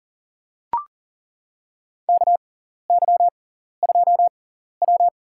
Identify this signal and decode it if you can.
Morse code, heard as a keyed tone, about 40 wpm, spaced as for 14 wpm, 700 Hz. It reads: KY2W
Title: USA Call Signs - 6x Character Spacing - 40wpm